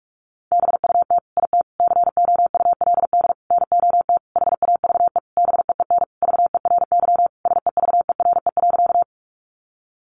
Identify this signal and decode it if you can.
Morse code, heard as a keyed tone, about 44 wpm, 700 Hz. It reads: BUT IT COULD NOT HAVE BEEN VERY SEVERE.
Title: Sentences from Top 1400 Words 45wpm